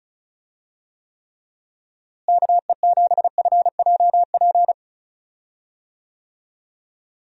Morse code, heard as a keyed tone, about 34 wpm, 700 Hz. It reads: KE7FJP
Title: USA Call Signs - 6 Characters - Repeated 3x - 35wpm